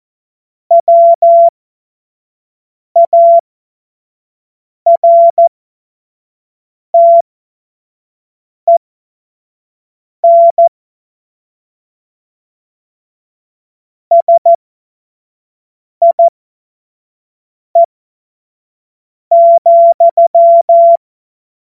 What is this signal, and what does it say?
Morse code, heard as a keyed tone, about 14 wpm, spaced as for 5 wpm, 680 Hz. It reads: WARTEN SIE,
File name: Morsekurs / Witze / Witz 16 April 2024 25 F70